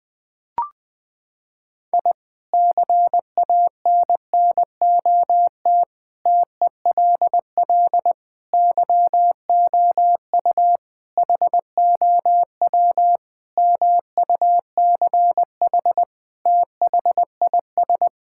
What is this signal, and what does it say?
Morse code, heard as a keyed tone, about 19 wpm, 700 Hz. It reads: I CANNOT TELL YOU HOW MUCH THIS
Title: Sentences with 5-Word Phrases - 20wpm